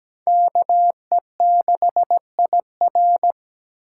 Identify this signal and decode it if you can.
Morse code, heard as a keyed tone, about 17 wpm, 700 Hz. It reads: KE6IR
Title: USA Call Signs - 2x Character Spacing - 17wpm